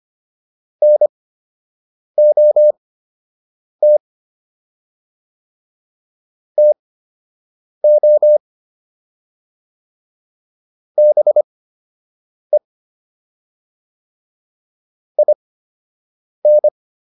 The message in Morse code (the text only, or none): NOT TO BE IN